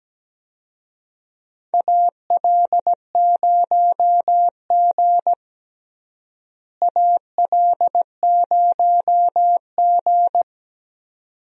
Morse code, heard as a keyed tone, about 17 wpm, 700 Hz. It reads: AL0G AL0G
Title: USA Call Signs - 4 Characters - Repeated 2x - 17wpm